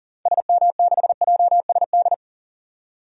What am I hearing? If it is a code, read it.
Morse code, heard as a keyed tone, about 40 wpm, 700 Hz. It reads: SM6JSD
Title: DXCC-100 Call Signs - 6 Characters - 40wpm